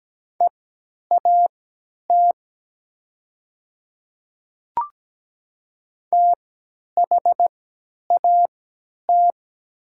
Morse code, heard as a keyed tone, about 17 wpm, spaced as for 10 wpm, 700 Hz. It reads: EAT THAT
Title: 3x Character Spacing - Top 300 Words - 17wpm